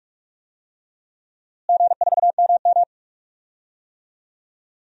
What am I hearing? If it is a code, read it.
Morse code, heard as a keyed tone, about 46 wpm, 700 Hz. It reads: G4MM
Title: DXCC-100 Call Signs - 4 Characters - Repeated 2x - 45wpm